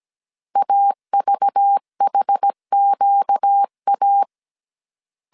Morse code, heard as a keyed tone, about 17 wpm, 800 Hz. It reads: AVHQA
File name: CW / DET / 1 9 1500C S A 16wpm 2828